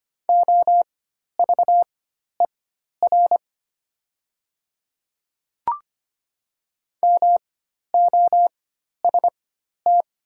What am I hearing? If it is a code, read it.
Morse code, heard as a keyed tone, about 25 wpm, spaced as for 12 wpm, 700 Hz. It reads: OVER MOST
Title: 4x Character Spacing - Top 100 Words - 25wpm